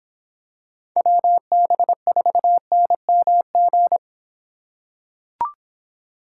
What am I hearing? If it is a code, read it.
Morse code, heard as a keyed tone, about 23 wpm, 700 Hz. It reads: WB4NMG